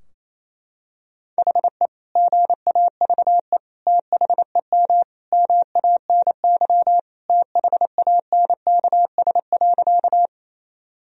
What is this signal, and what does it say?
Morse code, heard as a keyed tone, about 28 wpm, 700 Hz. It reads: HE GAVE THEM MANY THANKS.
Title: Sentences from Top 1000 Words - Review Part 3 - 28wpm